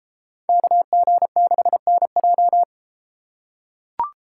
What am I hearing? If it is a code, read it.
Morse code, heard as a keyed tone, about 33 wpm, 700 Hz. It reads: KG6NJ